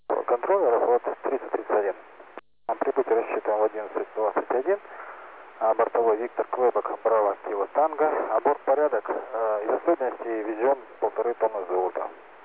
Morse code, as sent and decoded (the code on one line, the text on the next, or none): none